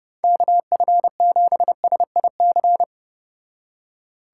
Morse code, heard as a keyed tone, about 31 wpm, 700 Hz. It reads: KF7SIC